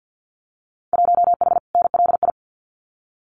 Morse code, heard as a keyed tone, about 50 wpm, 700 Hz. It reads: 1H NLI
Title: Winter Field Day Exchanges 50wpm